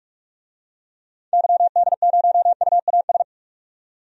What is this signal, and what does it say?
Morse code, heard as a keyed tone, about 46 wpm, 700 Hz. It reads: YD0UAS
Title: DXCC-100 Call Signs - 6 Characters - 45wpm